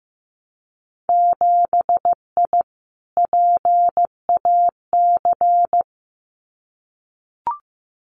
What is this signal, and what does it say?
Morse code, heard as a keyed tone, about 15 wpm, 700 Hz. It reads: 7I PAC
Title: Winter Field Day Exchanges Encore 15wpm